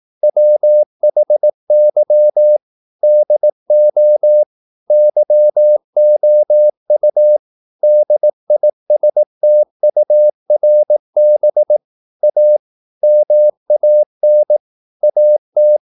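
Morse code, heard as a keyed tone, about 18 wpm, 600 Hz. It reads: WHY DO YOU DISTURB A MAN AT
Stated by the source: Morse practice at 18 wpm, English text, from CW med Gnister.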